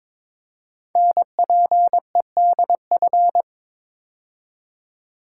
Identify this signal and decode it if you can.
Morse code, heard as a keyed tone, about 22 wpm, 700 Hz. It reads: NPEDF
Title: Code Groups - 5 Letter:Numbers - 22wpm